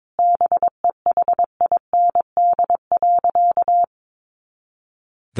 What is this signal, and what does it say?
Morse code, heard as a keyed tone, about 22 wpm, 700 Hz. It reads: BEHIND.
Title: A Sound Of Thunder 22wpm